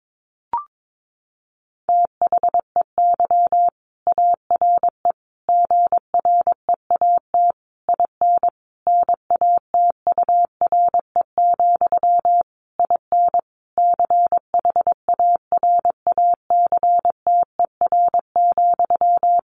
Morse code, heard as a keyed tone, about 22 wpm, 700 Hz. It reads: THEY ARE GREAT IN NATURE, IN CHARACTER,
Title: Sentences from Top 1400 Words 22wpm